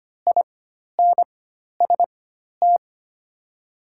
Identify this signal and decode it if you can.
Morse code, heard as a keyed tone, about 25 wpm, spaced as for 12 wpm, 700 Hz. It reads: INST
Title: 4x Character Spacing - Top 500 Words - 25wpm